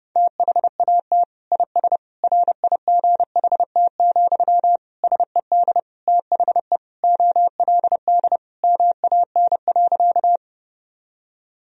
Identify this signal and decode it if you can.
Morse code, heard as a keyed tone, about 30 wpm, 700 Hz. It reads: THAT IS RIGHT, SED THE OLD MAN.
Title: Sentences from Top 200 Words and Abbreviations 30wpm